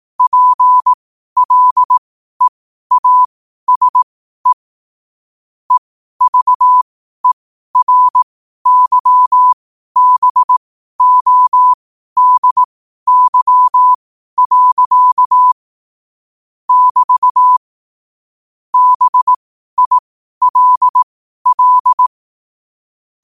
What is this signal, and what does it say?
Morse code, heard as a keyed tone, about 18 wpm, spaced as for 12 wpm, 1000 Hz. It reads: PLEASE EVERYBODY. <BT> BILL
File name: qotdfiles / 12wpm